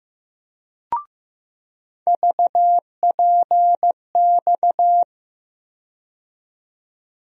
Morse code, heard as a keyed tone, about 15 wpm, 700 Hz. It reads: VPX
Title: USA Call Sign Postfixes - 3 Letters 15wpm